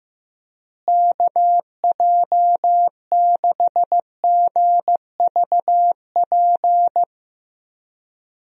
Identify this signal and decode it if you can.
Morse code, heard as a keyed tone, about 15 wpm, 700 Hz. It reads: KJ6GVP
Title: USA Call Signs - 6 Characters - Repeated 3x - 15wpm